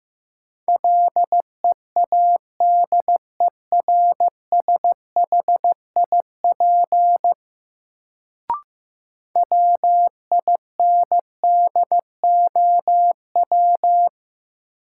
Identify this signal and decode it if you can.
Morse code, heard as a keyed tone, about 16 wpm, 700 Hz. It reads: LEADERSHIP WINDOW
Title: Sentences from Top 1600 Words - New Word Drill - 15wpm